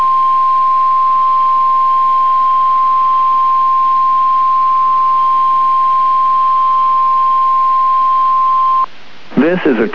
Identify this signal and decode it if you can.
Emergency Alert System (SAME) tones (the 1050 Hz attention tone), with speech. It is heard from 0.4 to 8.8 s.